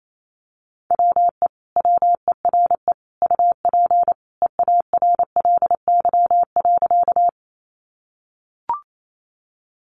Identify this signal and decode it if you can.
Morse code, heard as a keyed tone, about 28 wpm, 700 Hz. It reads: WE WERE UP EARLY.
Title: Sentences from Top 400 Words 28wpm